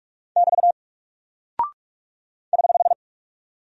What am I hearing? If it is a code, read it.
Morse code, heard as a keyed tone, about 44 wpm, 700 Hz. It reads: <BT> <HH>
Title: Single Character (Letter:Number:Punctuation:Prosign) Alt 45wpm